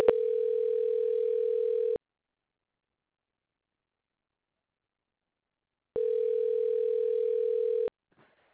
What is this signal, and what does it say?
Ringback tone (North America), 440 + 480 Hz.